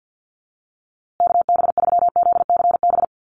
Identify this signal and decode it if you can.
Morse code, heard as a keyed tone, about 50 wpm, 700 Hz. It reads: KB3ZCB
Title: USA Call Signs Repeated 2x 50wpm